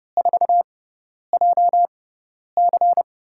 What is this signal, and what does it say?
Morse code, heard as a keyed tone, about 30 wpm, spaced as for 10 wpm, 700 Hz. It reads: 4JC
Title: USA Call Signs - 5 Characters - 6x Character Spacing - 30wpm